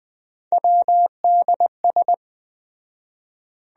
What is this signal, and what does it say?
Morse code, heard as a keyed tone, about 20 wpm, 700 Hz. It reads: WDS